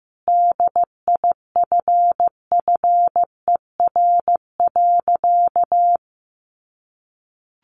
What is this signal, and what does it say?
Morse code, heard as a keyed tone, about 15 wpm, 700 Hz. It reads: DIFFER.
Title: Sentences from Top 1100 Words 15wpm